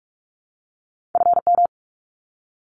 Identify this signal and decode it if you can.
Morse code, heard as a keyed tone, about 44 wpm, 700 Hz. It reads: FM
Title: Abbreviations 45wpm